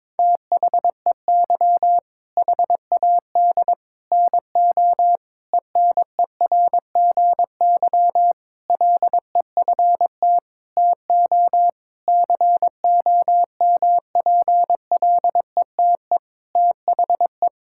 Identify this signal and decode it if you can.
Morse code, heard as a keyed tone, about 21 wpm, 700 Hz. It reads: THEY HAD NO ENERGY LEFT TO COMPLETE THE
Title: Sentences from Top 1000 Words - Review Part 6 - 22wpm